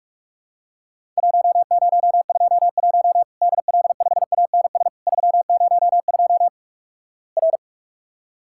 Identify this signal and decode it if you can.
Morse code, heard as a keyed tone, about 46 wpm, 700 Hz. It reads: 1021 DL5ANS 302 R
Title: Worked All Europe DX Contest - QTC Exchange - 45wpm